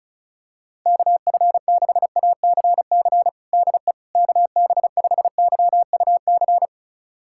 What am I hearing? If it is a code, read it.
Morse code, heard as a keyed tone, about 36 wpm, 700 Hz. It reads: KF6ACC DE KB5YUC
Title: USA Callsign DE Callsign 35wpm